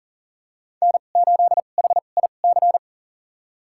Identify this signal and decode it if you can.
Morse code, heard as a keyed tone, about 39 wpm, spaced as for 29 wpm, 700 Hz. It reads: N8HIC
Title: USA Call Signs - 5 Characters - 2x Character Spacing - 40wpm